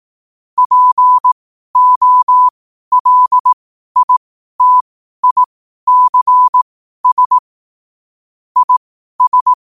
Morse code, heard as a keyed tone, about 18 wpm, spaced as for 12 wpm, 1000 Hz. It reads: POLITICS IS
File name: qotdfiles / 12wpm